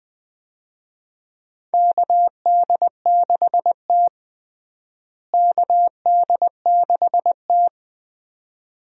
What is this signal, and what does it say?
Morse code, heard as a keyed tone, about 20 wpm, 700 Hz. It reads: KD6T KD6T
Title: USA Call Signs - 4 Characters - Repeated 2x - 20wpm